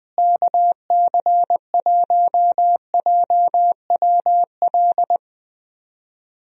Morse code, heard as a keyed tone, about 21 wpm, 700 Hz. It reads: KC1JWL
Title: USA Call Signs - 6 Characters - Repeated 2x - 20wpm